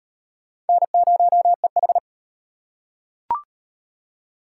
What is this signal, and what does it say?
Morse code, heard as a keyed tone, about 38 wpm, 700 Hz. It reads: N0EH